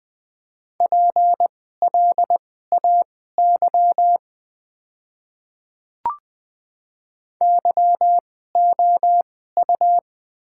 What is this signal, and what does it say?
Morse code, heard as a keyed tone, about 20 wpm, spaced as for 14 wpm, 700 Hz. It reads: PLAY YOU
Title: 2x Character Spacing - Top 200 Words - 20wpm